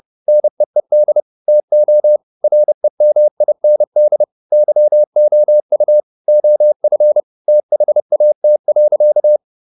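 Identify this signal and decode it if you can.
Morse code, heard as a keyed tone, about 30 wpm, 600 Hz. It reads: NEED TO REMIND YOU OF THAT.
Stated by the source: Morse code at 30 wpm, the novel Never by Ken Follett in English, from CW med Gnister.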